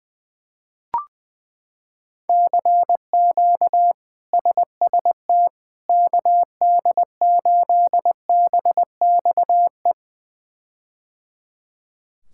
Morse code, heard as a keyed tone, about 21 wpm, 700 Hz. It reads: CQ SST KD8BXE
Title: K1USN SST QSO Practice - 20wpm